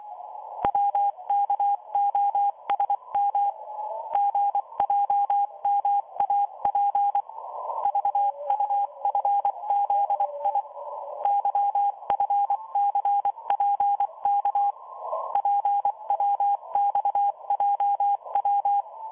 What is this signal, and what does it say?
Morse code, heard as a keyed tone, about 24 wpm, 810 Hz, with radio noise. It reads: WKOSM GJMAP VUFZI YFCPK PWXJW